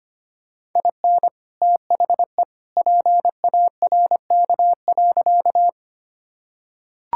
Morse code, heard as a keyed tone, about 25 wpm, 700 Hz. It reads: IN THE PARK.
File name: practice / 25wpm / Sentences with 2-Word Phrases - 25wpm